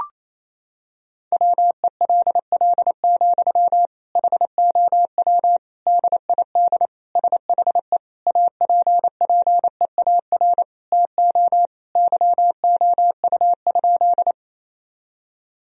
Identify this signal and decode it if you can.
Morse code, heard as a keyed tone, about 28 wpm, 700 Hz. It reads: WELL, HOW DID SHE APPEAR TO YOU?
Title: Sentences from Top 1000 Words - Review Part 2 - 28wpm